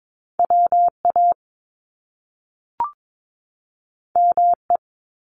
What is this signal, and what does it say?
Morse code, heard as a keyed tone, about 22 wpm, 700 Hz. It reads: WA ME
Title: Top Two Letter Combinations in Words 22wpm